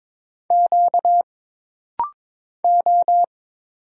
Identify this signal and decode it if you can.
Morse code, heard as a keyed tone, about 22 wpm, 700 Hz. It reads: Q O
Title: Single Letters - Rapid-Fire 22wpm